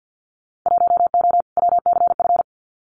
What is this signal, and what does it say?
Morse code, heard as a keyed tone, about 50 wpm, 700 Hz. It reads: 1O WCF
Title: Winter Field Day Exchanges 50wpm